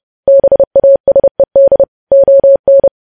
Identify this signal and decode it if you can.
Morse code, heard as a keyed tone, about 30 wpm, 550 Hz. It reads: BASED ON
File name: podcast / News headlines-1x-30x30